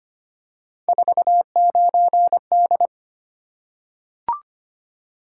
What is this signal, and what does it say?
Morse code, heard as a keyed tone, about 25 wpm, 700 Hz. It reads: 49D
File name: Three Letter-Number Combinations - 25wpm